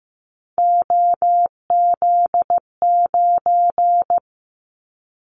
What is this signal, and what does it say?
Morse code, heard as a keyed tone, about 15 wpm, 700 Hz. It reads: OZ9
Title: Callsign Prefixes for Top 50 DXCC - 15wpm